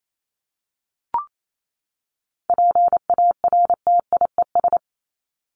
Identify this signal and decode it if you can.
Morse code, heard as a keyed tone, about 28 wpm, 700 Hz. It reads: PARTIES
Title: ES:S Suffix - Top 100 Words with Base Words 28wpm